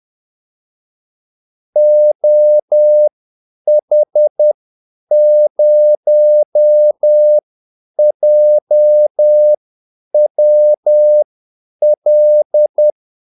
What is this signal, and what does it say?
Morse code, heard as a keyed tone, about 10 wpm, spaced as for 8 wpm, 600 Hz. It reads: OH0JWL
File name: Callsigns-10-8wpm